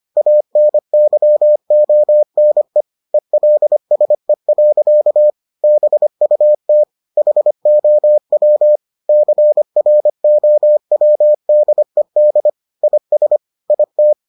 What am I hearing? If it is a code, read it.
Morse code, heard as a keyed tone, about 25 wpm, 600 Hz. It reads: ANYONE ELSE. BUT HOW CROWDED IS IT